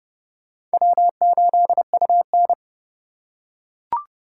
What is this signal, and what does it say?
Morse code, heard as a keyed tone, about 29 wpm, 700 Hz. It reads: W8UN